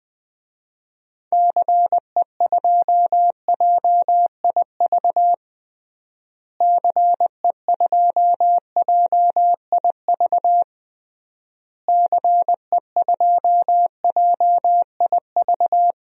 Morse code, heard as a keyed tone, about 20 wpm, 700 Hz. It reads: CE2JIV CE2JIV CE2JIV
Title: DXCC-100 Call Signs - 6 Characters - Repeated 3x - 20wpm